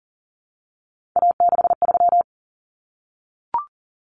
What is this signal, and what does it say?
Morse code, heard as a keyed tone, about 40 wpm, 700 Hz. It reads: A63
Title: Common Cut Numbers - 3-Digit 40wpm